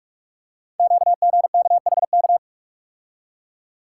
Morse code, heard as a keyed tone, about 44 wpm, 700 Hz. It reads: QGKHK
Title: Code Groups - 5 Letters - 45wpm